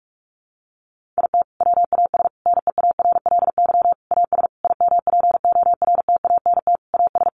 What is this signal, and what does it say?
Morse code, heard as a keyed tone, about 44 wpm, 700 Hz. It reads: IT WAS NEARLY AS IMPORTANT AS
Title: Sentences from Top 1400 Words 45wpm